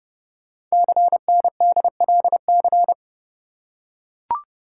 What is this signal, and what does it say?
Morse code, heard as a keyed tone, about 30 wpm, 690 Hz. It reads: CNDLC
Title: Code Groups - 5 Letters - 30wpm